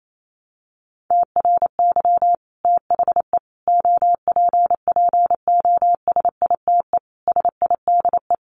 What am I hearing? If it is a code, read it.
Morse code, heard as a keyed tone, about 28 wpm, 700 Hz. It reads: TRY THE OPPOSITE SIDE
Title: Sentences from Top 1300 Words 28wpm